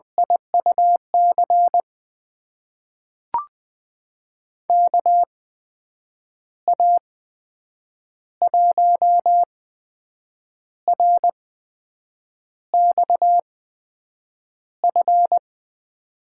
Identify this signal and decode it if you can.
Morse code, heard as a keyed tone, about 20 wpm, spaced as for 7 wpm, 700 Hz. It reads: IUC KA1RXF
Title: USA Call Signs - 6 Characters - 8x Character Spacing - 20wpm